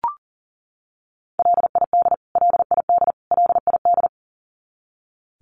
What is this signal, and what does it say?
Morse code, heard as a keyed tone, about 40 wpm, 700 Hz. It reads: LID LID LID
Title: Abbreviations Repeated 3x - 40wpm